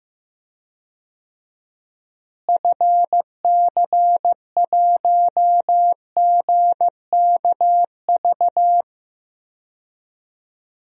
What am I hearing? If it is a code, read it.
Morse code, heard as a keyed tone, about 15 wpm, 700 Hz. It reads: FC1GKV
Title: DXCC-100 Call Signs - 6 Characters - Repeated 2x - 15wpm